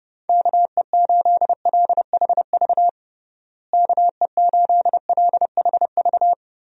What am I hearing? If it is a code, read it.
Morse code, heard as a keyed tone, about 32 wpm, 700 Hz. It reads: KE8LHV KE8LHV
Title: USA Call Signs - 6 Characters - Repeated 3x - 30wpm